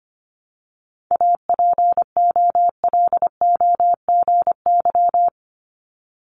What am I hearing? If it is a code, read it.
Morse code, heard as a keyed tone, about 25 wpm, 700 Hz. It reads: APOLOGY
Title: Multi-letter Phonemes 25wpm